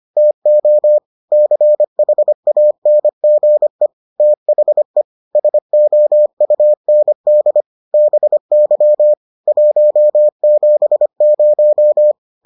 Morse code, heard as a keyed tone, about 25 wpm, 600 Hz. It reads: TO CHANGE THE SOUND BY 170